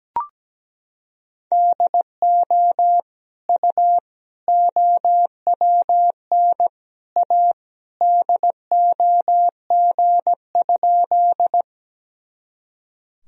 Morse code, heard as a keyed tone, about 17 wpm, 700 Hz. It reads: DO U OWN A DOG?
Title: Sentences from Top 400 Words and Abbreviations 17wpm